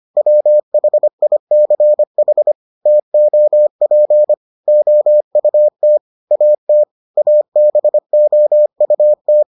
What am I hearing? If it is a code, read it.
Morse code, heard as a keyed tone, about 25 wpm, 600 Hz. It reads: WHICH TOP OUT AT ABOUT